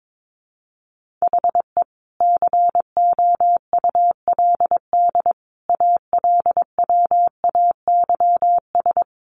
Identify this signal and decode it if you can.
Morse code, heard as a keyed tone, about 22 wpm, 700 Hz. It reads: HE COULD ALWAYS
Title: Sentences from Top 1400 Words 22wpm